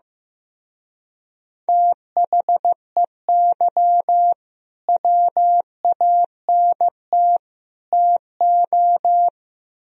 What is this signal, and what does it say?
Morse code, heard as a keyed tone, about 17 wpm, 700 Hz. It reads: THEY WANT TO
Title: Speed-Racing - Sets of 3 Words 15wpm